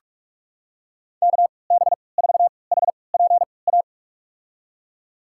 Morse code, heard as a keyed tone, about 46 wpm, spaced as for 25 wpm, 700 Hz. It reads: KB4HPA
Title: USA Call Signs - 3x Character Spacing - 45wpm